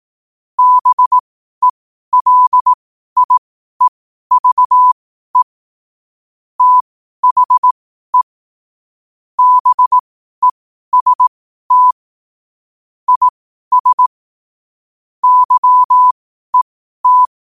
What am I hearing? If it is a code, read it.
Morse code, heard as a keyed tone, about 18 wpm, spaced as for 12 wpm, 1000 Hz. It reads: BELIEVE THE BEST IS YET
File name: qotdfiles / 12wpm